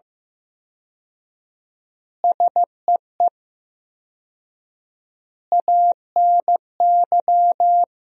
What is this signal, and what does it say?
Morse code, heard as a keyed tone, about 15 wpm, 700 Hz. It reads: SEE ANY
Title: 4x Word-Spacing - Sentences from Top 500 Words - 15wpm